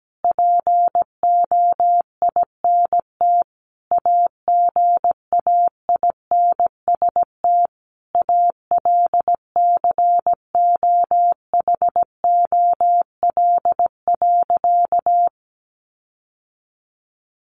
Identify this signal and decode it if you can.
Morse code, heard as a keyed tone, about 17 wpm, 700 Hz. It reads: POINT AGAINST ALCOHOL.
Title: Sentences from Top 1400 Words 17wpm